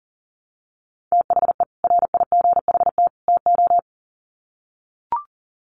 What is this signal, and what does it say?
Morse code, heard as a keyed tone, about 40 wpm, 700 Hz. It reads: THE RIGHT TO
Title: Sets of 3 Words 40wpm